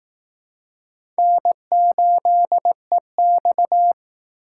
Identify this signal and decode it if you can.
Morse code, heard as a keyed tone, about 18 wpm, 700 Hz. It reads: N8EX
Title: Speed Racing - USA 1x2 Call Signs 15wpm